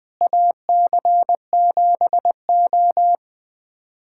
Morse code, heard as a keyed tone, about 21 wpm, 700 Hz. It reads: AC7O
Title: USA Callsign DE Callsign - 4 Characters - 20wpm